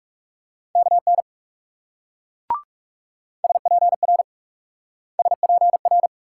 Morse code, heard as a keyed tone, about 39 wpm, 700 Hz. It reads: KN SPR SPR